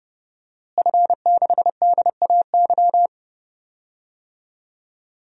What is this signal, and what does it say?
Morse code, heard as a keyed tone, about 30 wpm, 700 Hz. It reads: F6DAY
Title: DXCC-100 Call Signs - 5 Characters - Repeated 3x - 30wpm